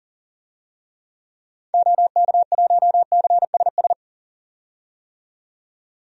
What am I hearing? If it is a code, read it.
Morse code, heard as a keyed tone, about 40 wpm, 700 Hz. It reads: OK1CSS
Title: DXCC-100 Call Signs - 6 Characters - 40wpm